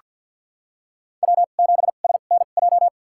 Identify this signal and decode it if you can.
Morse code, heard as a keyed tone, about 46 wpm, spaced as for 35 wpm, 700 Hz. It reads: W7SNJ
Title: USA Call Signs - 5 Characters - 2x Character Spacing - 50wpm